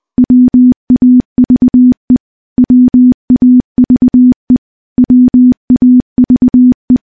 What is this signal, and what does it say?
Morse code, heard as a keyed tone, about 20 wpm, 250 Hz. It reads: WAVE WAVE WAVE
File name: morse-training-20WPM